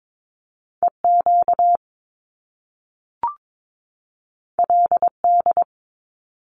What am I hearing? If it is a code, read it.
Morse code, heard as a keyed tone, about 22 wpm, 700 Hz. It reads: EQ LD
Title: Top Two Letter Combinations in Words 22wpm